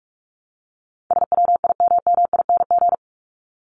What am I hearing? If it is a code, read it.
Morse code, heard as a keyed tone, about 44 wpm, 700 Hz. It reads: SWIMMING
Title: Top 1501-1750 Words 45wpm